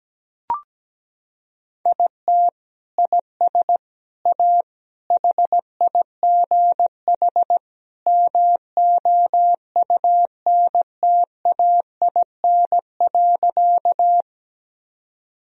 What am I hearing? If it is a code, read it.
Morse code, heard as a keyed tone, about 17 wpm, 700 Hz. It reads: IT IS A HIGH MOUNTAIN.
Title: Sentences from Top 1000 Words - Review Part 1 - 17wpm